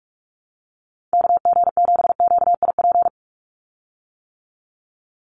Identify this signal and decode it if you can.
Morse code, heard as a keyed tone, about 44 wpm, 700 Hz. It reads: KG7QIP
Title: USA Call Signs 45wpm